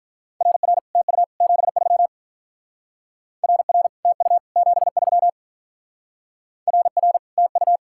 Morse code, heard as a keyed tone, about 44 wpm, 700 Hz. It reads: RR TU 73 RR TU 73 RR TU